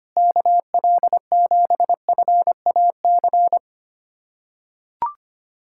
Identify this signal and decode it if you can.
Morse code, heard as a keyed tone, about 25 wpm, 700 Hz. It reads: KL7FAC
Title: USA Callsign DE Callsign 25wpm